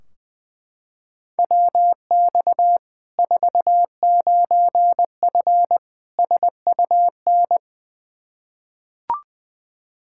Morse code, heard as a keyed tone, about 20 wpm, 700 Hz. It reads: WX 49F SUN
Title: Practice QSO - Exchange 2 - Info - 20wpm